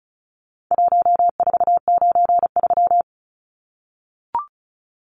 Morse code, heard as a keyed tone, about 34 wpm, 700 Hz. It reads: 1493
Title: Numbers - Contest Style - 4-Digit 35wpm